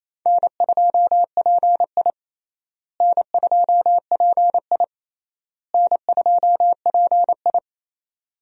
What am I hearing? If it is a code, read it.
Morse code, heard as a keyed tone, about 27 wpm, 700 Hz. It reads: N2PI N2PI N2PI
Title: USA Call Signs - 4 Characters - Repeated 3x - 28wpm